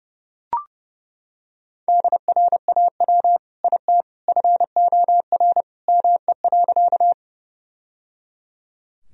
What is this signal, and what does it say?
Morse code, heard as a keyed tone, about 30 wpm, 700 Hz. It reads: DRAW IT FOR ME.
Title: Sentences from Top 1000 Words - Review Part 1 - 30wpm